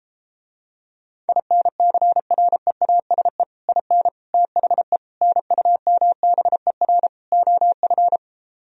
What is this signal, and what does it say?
Morse code, heard as a keyed tone, about 33 wpm, 700 Hz. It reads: INCREASE IN THE NUMBER OF